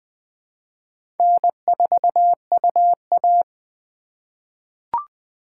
Morse code, heard as a keyed tone, about 20 wpm, 700 Hz. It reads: N4UA
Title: USA Call Signs - 4 Characters - 4x Character Spacing - 20wpm